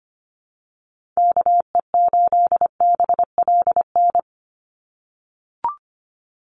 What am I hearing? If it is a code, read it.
Morse code, heard as a keyed tone, about 25 wpm, 700 Hz. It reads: KE8BLN
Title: USA Call Signs 25wpm